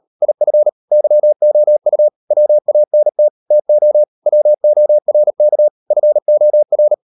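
Morse code, heard as a keyed tone, about 38 wpm, 600 Hz. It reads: IF YOU WANT TO WORK FOR
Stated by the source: Morse code practice at 38 WPM, sending Ken Follett's novel Never in English, from CW med Gnister.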